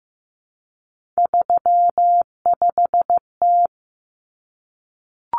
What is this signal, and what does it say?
Morse code, heard as a keyed tone, about 15 wpm, 700 Hz. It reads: 35T
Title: Common Cut Numbers - 3-Digit 15wpm